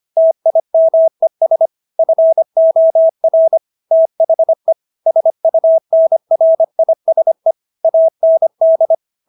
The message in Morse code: TIMES FOR THE SUNRISE AND

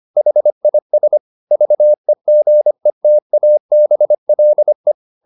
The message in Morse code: HIS VEGETABLE

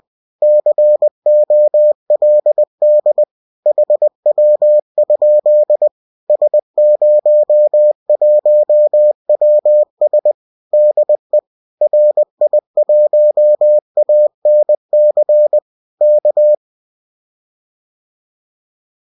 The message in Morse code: COLD HW? S01WS DE RI1ANC K

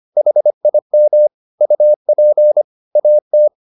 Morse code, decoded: HIM UP AT

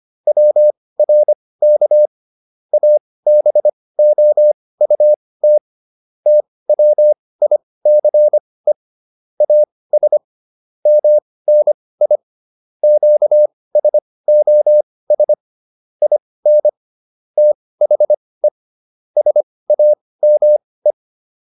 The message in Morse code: WRK ABOUT TWICE AS MNI QSOS IN THE SAME

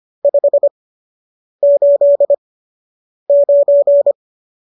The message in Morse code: 589

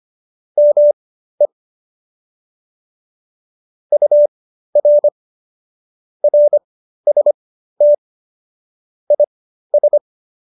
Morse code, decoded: ME UR RST IS

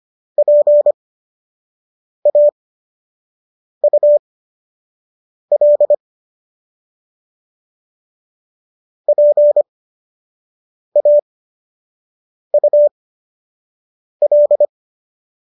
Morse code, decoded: PAUL PAUL

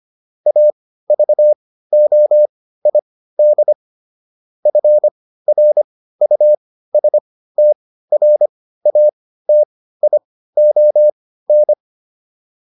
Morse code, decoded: AVOID FRUSTRATION

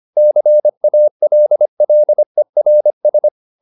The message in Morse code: CALLERS

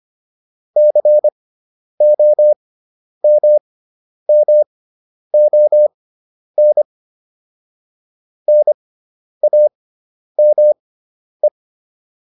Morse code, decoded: COMMON NAME

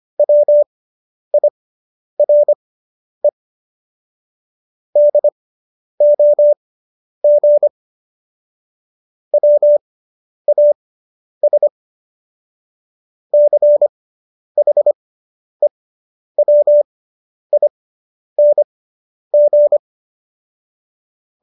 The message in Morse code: WIRE DOG WAS CHEWING

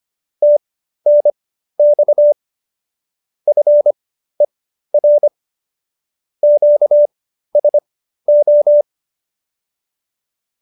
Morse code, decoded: TNX FER QSO

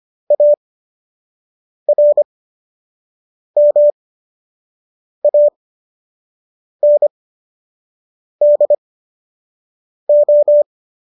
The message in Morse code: ARMANDO

